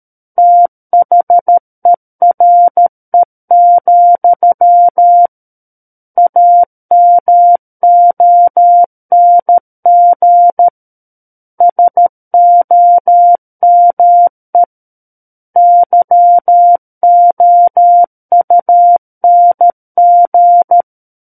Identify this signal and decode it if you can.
Morse code, heard as a keyed tone, about 13 wpm, 700 Hz. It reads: THERE, AMONG SOME YOUNG
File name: Book One / 04-Chapter 4 - 13 WPM